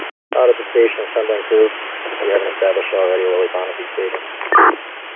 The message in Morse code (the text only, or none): none